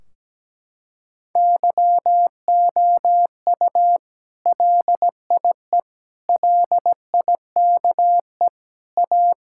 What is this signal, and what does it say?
Morse code, heard as a keyed tone, about 17 wpm, 700 Hz. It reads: YOU LIE LIKE A
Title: Sentences from Top 1600 Words 17wpm